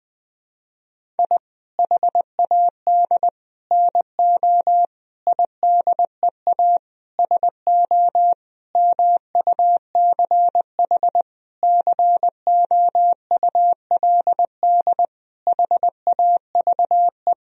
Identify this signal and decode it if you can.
Morse code, heard as a keyed tone, about 20 wpm, 700 Hz. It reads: I HAD NO IDEA SO MUCH COULD HAVE
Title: Sentences from Top 1000 Words - Review Part 4 - 20wpm